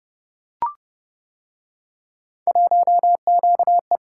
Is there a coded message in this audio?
Yes: Morse code from 2.5 to 4.0 s, heard as a keyed tone, about 29 wpm, 700 Hz. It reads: 1QE